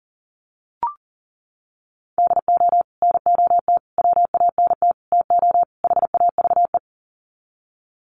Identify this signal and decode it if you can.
Morse code, heard as a keyed tone, about 40 wpm, 700 Hz. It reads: DO NOT WANT TO HAVE
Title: Sets of 5 Words 40wpm